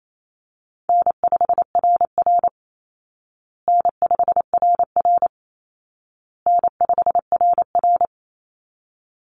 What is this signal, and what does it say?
Morse code, heard as a keyed tone, about 28 wpm, 700 Hz. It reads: N5RR N5RR N5RR
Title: Call Signs Repeated 2x - 28wpm